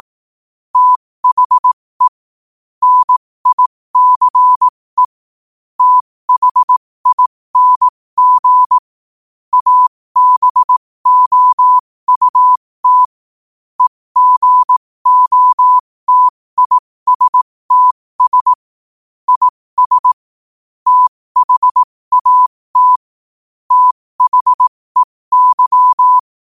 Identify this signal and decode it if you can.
Morse code, heard as a keyed tone, about 18 wpm, spaced as for 15 wpm, 1000 Hz. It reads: THE NICE THING ABOUT EGOTISTS IS THAT THEY
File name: qotdfiles / 15wpm